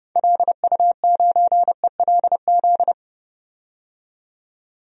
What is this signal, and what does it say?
Morse code, heard as a keyed tone, about 30 wpm, 700 Hz. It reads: LU9ELZ
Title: DXCC-100 Call Signs - 6 Characters - 30wpm